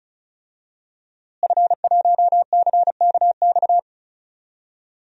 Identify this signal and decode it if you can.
Morse code, heard as a keyed tone, about 34 wpm, 700 Hz. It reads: F1CKX
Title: DXCC-100 Call Signs - 5 Characters - Repeated 2x - 35wpm